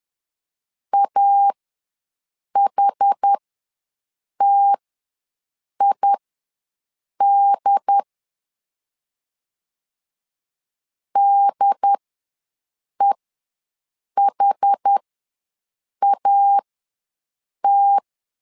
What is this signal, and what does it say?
Morse code, heard as a keyed tone, about 10 wpm, spaced as for 6 wpm, 800 Hz. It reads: AHTID DEHAT
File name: LEZIONI / 2 UHDN / 2 UHD ws 10wpm 2058